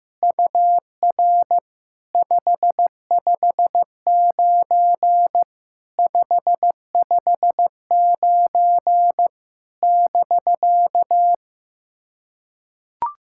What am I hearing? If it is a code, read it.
Morse code, heard as a keyed tone, about 16 wpm, 700 Hz. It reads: UR 559 559 <BK>
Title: SOTA QSO Practice 15wpm